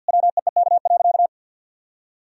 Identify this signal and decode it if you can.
Morse code, heard as a keyed tone, about 50 wpm, 700 Hz. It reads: WEEK.
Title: Sentences from Top 1000 Words - Review Part 4 - 50wpm